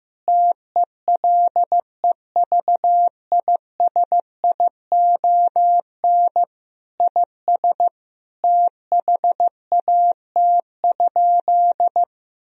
Morse code, heard as a keyed tone, about 16 wpm, 700 Hz. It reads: TELEVISION IS THAT?
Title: Sentences from Top 1000 Words - Review Part 4 - 15wpm